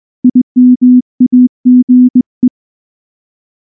Morse code, heard as a keyed tone, about 20 wpm, 250 Hz. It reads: IMAGE
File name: morse / morse-training-20WPM-2